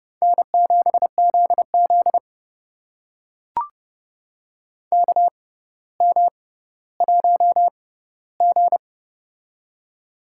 Morse code, heard as a keyed tone, about 30 wpm, spaced as for 15 wpm, 700 Hz. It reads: N7ZZ KM1G
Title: USA Call Signs - 4 Characters - 6x Character Spacing - 30wpm